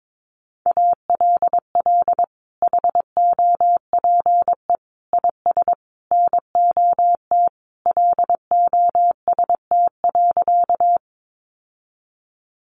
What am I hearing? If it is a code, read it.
Morse code, heard as a keyed tone, about 22 wpm, 700 Hz. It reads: ALL HOPE IS NOT LOST.
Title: Sentences from Top 1400 Words 22wpm